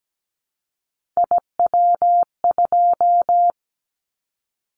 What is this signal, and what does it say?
Morse code, heard as a keyed tone, about 17 wpm, 700 Hz. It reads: IW2
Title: Callsign Prefixes for Top 25 DXCC - 17wpm